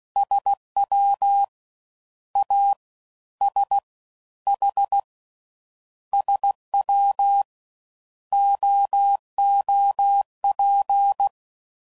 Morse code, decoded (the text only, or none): SWASH SWOOP